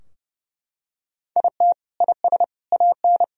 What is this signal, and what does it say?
Morse code, heard as a keyed tone, about 30 wpm, 700 Hz. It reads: IT IS AN
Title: Sentences from Top 1000 Words - Review Part 4 - 30wpm